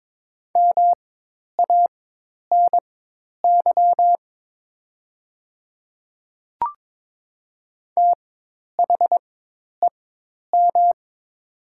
Morse code, heard as a keyed tone, about 22 wpm, spaced as for 10 wpm, 700 Hz. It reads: MANY THEM
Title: 4x Character Spacing - Top 100 Words - 22wpm